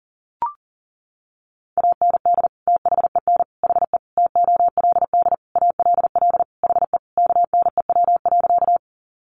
Morse code, heard as a keyed tone, about 40 wpm, 700 Hz. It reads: AND THEN HE TOLD ALL HE KNEW.
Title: Sentences from Top 400 Words 40wpm